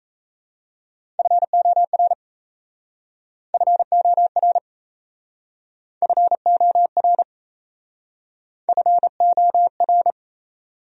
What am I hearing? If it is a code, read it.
Morse code, heard as a keyed tone, about 38 wpm, 700 Hz. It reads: FOR FOR FOR FOR